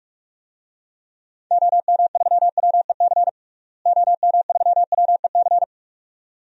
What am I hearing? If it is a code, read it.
Morse code, heard as a keyed tone, about 46 wpm, 700 Hz. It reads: OM3WEC OM3WEC
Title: DXCC-100 Call Signs - 6 Characters - Repeated 2x - 45wpm